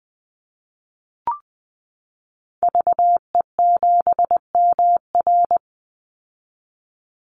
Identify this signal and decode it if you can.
Morse code, heard as a keyed tone, about 20 wpm, 700 Hz. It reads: VE7MR
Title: Call Signs Repeated 2x - 20wpm